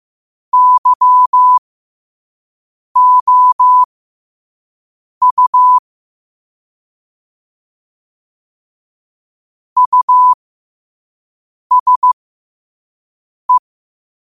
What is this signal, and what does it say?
Morse code, heard as a keyed tone, about 15 wpm, spaced as for 5 wpm, 1000 Hz. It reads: YOU USE
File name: qotdfiles / 5wpm